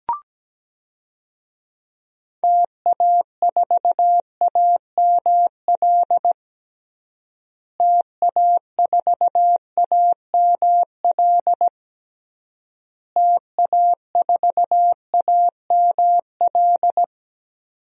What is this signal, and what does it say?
Morse code, heard as a keyed tone, about 17 wpm, 700 Hz. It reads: TA4AML TA4AML TA4AML
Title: DXCC-100 Call Signs - 6 Characters - Repeated 3x - 17wpm